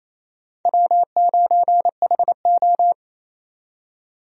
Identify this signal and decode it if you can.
Morse code, heard as a keyed tone, about 28 wpm, 700 Hz. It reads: W9HO
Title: USA Call Signs - 4 Characters - 28wpm